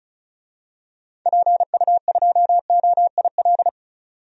Morse code, heard as a keyed tone, about 36 wpm, 700 Hz. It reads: PU2OIL
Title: DXCC-100 Call Signs - 6 Characters - 35wpm